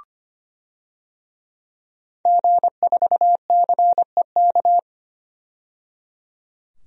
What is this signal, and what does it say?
Morse code, heard as a keyed tone, about 25 wpm, 700 Hz. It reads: G4CEK
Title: DXCC-100 Call Signs - 5 Characters - 25wpm